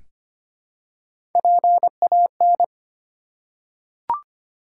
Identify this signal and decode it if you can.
Morse code, heard as a keyed tone, about 25 wpm, 700 Hz. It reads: PAN
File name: practice / 25wpm / Easy Words with Hints - 25wpm